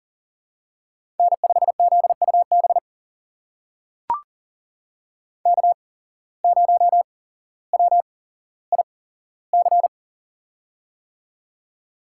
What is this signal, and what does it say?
Morse code, heard as a keyed tone, about 39 wpm, spaced as for 17 wpm, 700 Hz. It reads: N5ZUB K0WIC
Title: USA Call Signs - 5 Characters - 8x Character Spacing - 40wpm